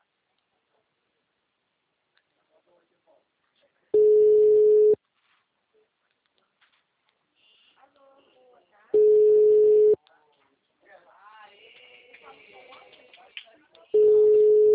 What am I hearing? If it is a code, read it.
Ringback tone (Europe, 425 Hz), with speech.